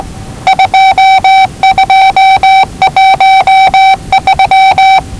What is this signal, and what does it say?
Morse code, heard as a keyed tone, about 20 wpm, 770 Hz. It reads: 2213
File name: WellingtonRadioZlwCloseDownOnHfAndMf / 01-Zlw 4277khz 417.5 qrt until 0925z